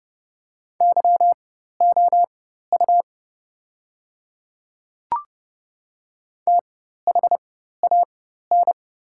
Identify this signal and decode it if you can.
Morse code, heard as a keyed tone, about 30 wpm, spaced as for 14 wpm, 700 Hz. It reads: YOU THAN
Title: 4x Character Spacing - Top 100 Words - 30wpm